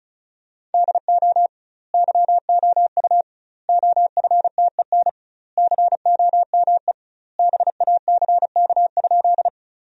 Morse code, heard as a keyed tone, about 36 wpm, 700 Hz. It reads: DO YOU OFTEN COME BACK?
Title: 2x Word-Spacing - Sentences from Top 300 Words - 35wpm